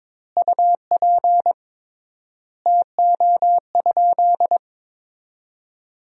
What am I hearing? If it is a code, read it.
Morse code, heard as a keyed tone, about 22 wpm, 700 Hz. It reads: UP TO?
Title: 3x Word-Spacing - Sentences from Top 100 Words - 22wpm